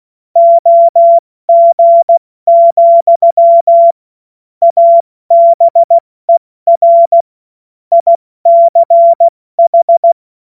Morse code, heard as a keyed tone, about 16 wpm, 680 Hz. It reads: OG, ABER ICH
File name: Morsekurs / Witze / Witz 14.November 2023 70 F80